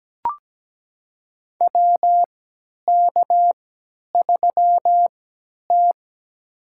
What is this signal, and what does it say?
Morse code, heard as a keyed tone, about 17 wpm, spaced as for 10 wpm, 700 Hz. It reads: WK3T
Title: USA Call Signs - 4 Characters - 3x Character Spacing - 17wpm